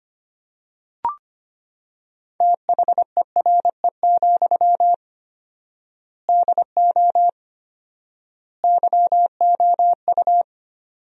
Morse code, heard as a keyed tone, about 25 wpm, 700 Hz. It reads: THERE, DO YOU
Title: 4x Word-Spacing - Sentences from Top 300 Words - 25wpm